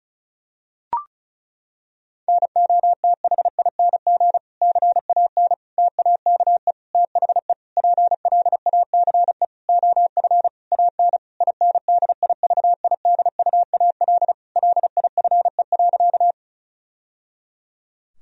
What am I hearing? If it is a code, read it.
Morse code, heard as a keyed tone, about 34 wpm, 700 Hz. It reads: NOTHING CAN TAKE THE PLACE OF AN INDIVIDUAL LIFE.
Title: Sentences from Top 1000 Words - Review Part 4 - 35wpm